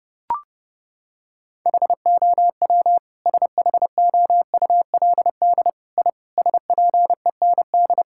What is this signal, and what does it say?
Morse code, heard as a keyed tone, about 30 wpm, 700 Hz. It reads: HOW SHOULD I SPEND
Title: Sentences from Top 1000 Words - Review Part 5 - 30wpm